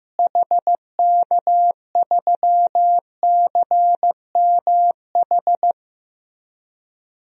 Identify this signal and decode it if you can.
Morse code, heard as a keyed tone, about 15 wpm, 700 Hz. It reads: HK3CMH
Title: DXCC-100 Call Signs - 6 Characters - Repeated 3x - 15wpm